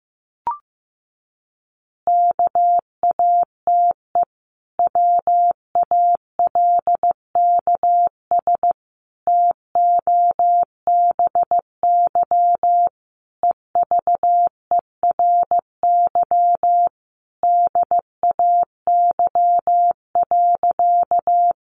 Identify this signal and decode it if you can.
Morse code, heard as a keyed tone, about 15 wpm, 700 Hz. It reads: KATE WALKS TOBY EVERY DAY.
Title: Easy Phrases 15wpm